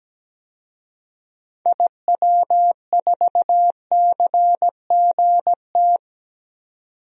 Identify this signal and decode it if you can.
Morse code, heard as a keyed tone, about 17 wpm, 700 Hz. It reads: IW4CGT
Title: DXCC-100 Call Signs - 6 Characters - 17wpm